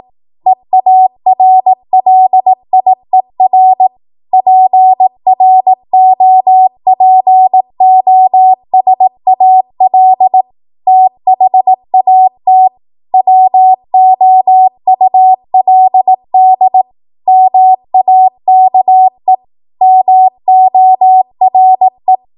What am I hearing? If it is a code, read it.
Morse code, heard as a keyed tone, about 18 wpm, 750 Hz. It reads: EARLIER PROPOSAL THAT WOULD MAKE MORE
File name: Morse / 18WPM